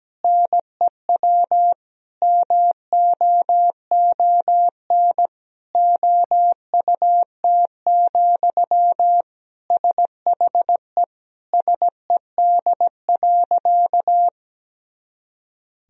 Morse code, heard as a keyed tone, about 17 wpm, 700 Hz. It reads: NEW MOON OUT, SHE SED.
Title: Sentences from Top 500 Words and Abbreviations 17wpm